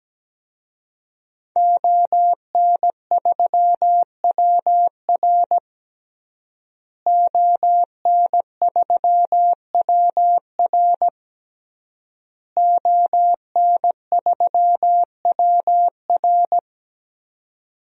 Morse code, heard as a keyed tone, about 17 wpm, 700 Hz. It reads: ON3WR ON3WR ON3WR
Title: DXCC-100 Call Signs - 5 Characters - Repeated 3x - 17wpm